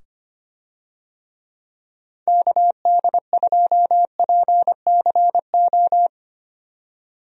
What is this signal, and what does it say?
Morse code, heard as a keyed tone, about 25 wpm, 700 Hz. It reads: KD2PCO
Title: USA Call Signs - 6 Characters - 25wpm